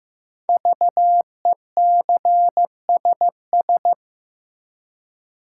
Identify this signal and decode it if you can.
Morse code, heard as a keyed tone, about 15 wpm, 700 Hz. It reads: VECSS
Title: Code Groups - 5 Letters - 15wpm